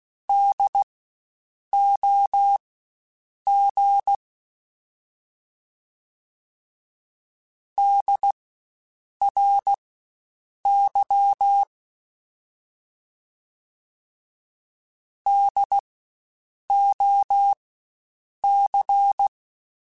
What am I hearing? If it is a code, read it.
Morse code, heard as a keyed tone, about 16 wpm, spaced as for 7 wpm, 800 Hz. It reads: DOG DRY DOC
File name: Spaced Alphas / Spaced Alphas mp3s / Ds sp @15